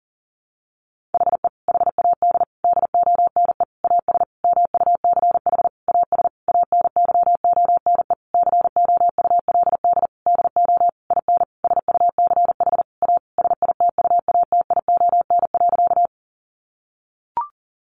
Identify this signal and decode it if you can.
Morse code, heard as a keyed tone, about 40 wpm, 700 Hz. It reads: HE HAD DONE AS MUCH AS ANYONE COULD DO IN SUCH A SITUATION.
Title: Sentences from Top 1200 Words 40wpm